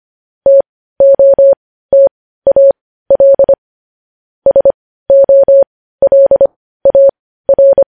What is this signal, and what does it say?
Morse code, heard as a keyed tone, about 25 wpm, spaced as for 15 wpm, 550 Hz. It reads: TOTAL SOLAR
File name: podcast / News headlines-1x-25x15